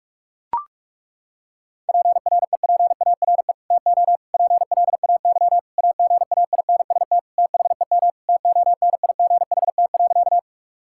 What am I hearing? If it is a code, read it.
Morse code, heard as a keyed tone, about 41 wpm, 700 Hz. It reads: PREPARE TO PLAY AGAINST THEM TONIGHT.
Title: Sentences from Top 1000 Words - Review Part 6 - 45wpm